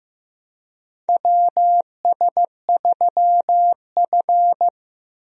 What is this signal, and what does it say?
Morse code, heard as a keyed tone, about 15 wpm, 700 Hz. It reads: WS3F
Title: USA Call Signs - 4 Characters - Repeated 2x - 15wpm